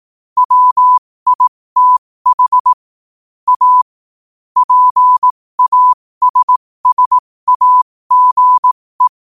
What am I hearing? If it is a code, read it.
Morse code, heard as a keyed tone, about 18 wpm, spaced as for 15 wpm, 1000 Hz. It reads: WITH A PASSAGE
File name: qotdfiles / 15wpm